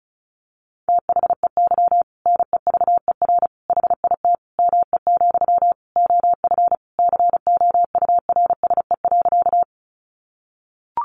Morse code, heard as a keyed tone, about 34 wpm, 700 Hz. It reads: THEY NEVER HIT ME, OF COURSE.
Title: Sentences from Top 1200 Words 35wpm